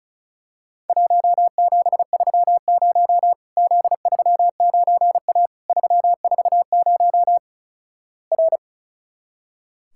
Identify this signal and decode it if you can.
Morse code, heard as a keyed tone, about 36 wpm, 700 Hz. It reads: 1730 Z39A 340 R
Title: Worked All Europe DX Contest - QTC Exchange - 35wpm